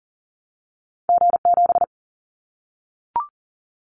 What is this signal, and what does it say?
Morse code, heard as a keyed tone, about 40 wpm, 700 Hz. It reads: G7
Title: Callsign Prefixes for Top 25 DXCC - 40wpm